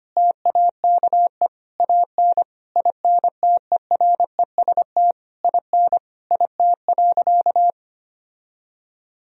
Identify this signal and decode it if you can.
Morse code, heard as a keyed tone, about 25 wpm, 700 Hz. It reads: TAKE AN INTEREST IN IT.
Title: Sentences from Top 1000 Words - Review Part 2 - 25wpm